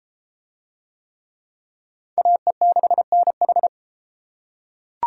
Morse code, heard as a keyed tone, about 31 wpm, 700 Hz. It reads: AE6NH